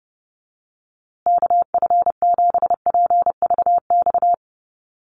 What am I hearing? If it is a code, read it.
Morse code, heard as a keyed tone, about 30 wpm, 700 Hz. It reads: KF7PVX
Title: USA Call Signs Repeated 3x 30wpm